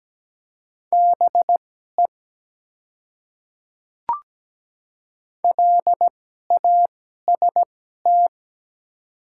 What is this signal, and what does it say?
Morse code, heard as a keyed tone, about 17 wpm, spaced as for 12 wpm, 700 Hz. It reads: BE LAST
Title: 2x Character Spacing - Top 300 Words - 17wpm